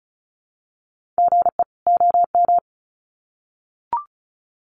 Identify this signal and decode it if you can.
Morse code, heard as a keyed tone, about 34 wpm, 700 Hz. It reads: GE OM
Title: Common QSO Phrases 35wpm